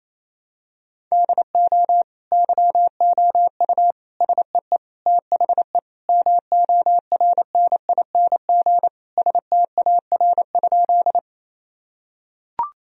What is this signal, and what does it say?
Morse code, heard as a keyed tone, about 28 wpm, 700 Hz. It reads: DO YOU SEE THE MORNING STAR?
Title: Sentences from Top 1000 Words - Review Part 2 - 28wpm